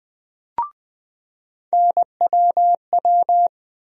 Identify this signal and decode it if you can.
Morse code, heard as a keyed tone, about 20 wpm, 700 Hz. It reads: NWW
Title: USA Call Sign Postfixes - 3 Letters 20wpm